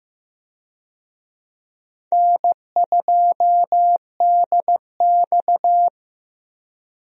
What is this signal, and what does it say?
Morse code, heard as a keyed tone, about 15 wpm, 700 Hz. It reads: N2DX
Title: USA Call Signs - 4 Characters - Repeated 3x - 15wpm